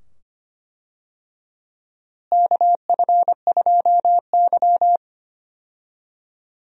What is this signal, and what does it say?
Morse code, heard as a keyed tone, about 25 wpm, 700 Hz. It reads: KF2Y
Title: USA Call Signs - 4 Characters - Repeated 2x - 25wpm